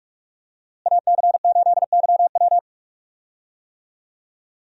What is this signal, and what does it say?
Morse code, heard as a keyed tone, about 45 wpm, 700 Hz. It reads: AC8YW
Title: USA Call Signs - 5 Characters - Repeated 2x - 45wpm